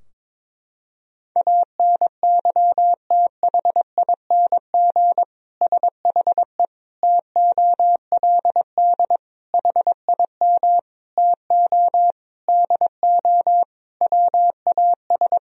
Morse code, heard as a keyed tone, about 21 wpm, 700 Hz. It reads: ANYTHING SHE TOLD HIM TO DO WAS
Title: Sentences from Top 1000 Words - Review Part 3 - 22wpm